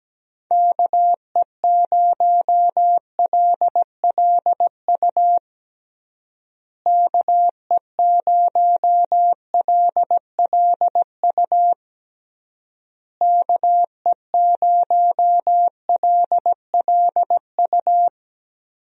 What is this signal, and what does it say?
Morse code, heard as a keyed tone, about 17 wpm, 700 Hz. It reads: KE0LLU KE0LLU KE0LLU
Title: USA Call Signs - 6 Characters - Repeated 3x - 17wpm